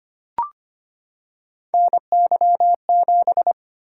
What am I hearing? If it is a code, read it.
Morse code, heard as a keyed tone, about 25 wpm, 700 Hz. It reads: NY7
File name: Three Letter-Number Combinations - 25wpm